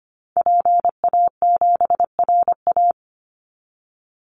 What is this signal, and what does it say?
Morse code, heard as a keyed tone, about 25 wpm, 700 Hz. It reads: PA7RA
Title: Call Signs Repeated 3x - 25wpm